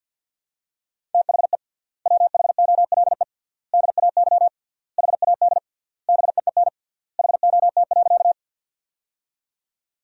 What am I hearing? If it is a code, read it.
Morse code, heard as a keyed tone, about 53 wpm, 700 Hz. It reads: THE WHOLE DAY HAD BEEN HOT.
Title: 3x Word-Spacing - Sentences from Top 500 Words - 50wpm